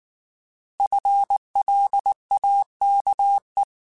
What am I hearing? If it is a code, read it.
Morse code, heard as a keyed tone, about 19 wpm, 800 Hz. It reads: FLAKE
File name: Alphabetical / Alphabetical mp3s / Fs @18